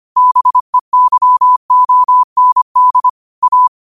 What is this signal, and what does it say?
Morse code, heard as a keyed tone, about 25 wpm, 1000 Hz. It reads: BEYOND A
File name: qotdfiles / 25wpm